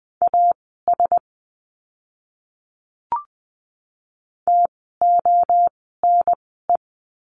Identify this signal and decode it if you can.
Morse code, heard as a keyed tone, about 20 wpm, spaced as for 7 wpm, 700 Hz. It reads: A STONE
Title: 036 Lesson 7f - Letter s - Words 20wpm